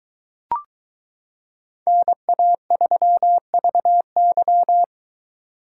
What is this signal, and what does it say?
Morse code, heard as a keyed tone, about 22 wpm, 700 Hz. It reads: NA3VY